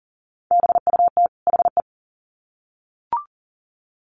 Morse code, heard as a keyed tone, about 40 wpm, 700 Hz. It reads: BUT HE
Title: Sets of 2 Words 40wpm